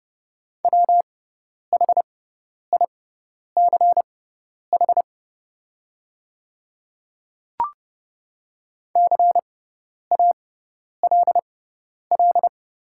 Morse code, heard as a keyed tone, about 30 wpm, spaced as for 10 wpm, 700 Hz. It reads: WHICH CALL
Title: 6x Character Spacing - Top 100 Words - 30wpm